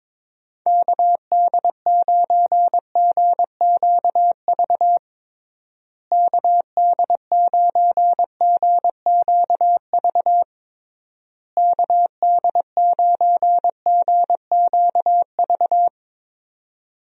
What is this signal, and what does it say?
Morse code, heard as a keyed tone, about 22 wpm, 700 Hz. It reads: KD9GQV KD9GQV KD9GQV
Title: USA Call Signs - 6 Characters - Repeated 3x - 22wpm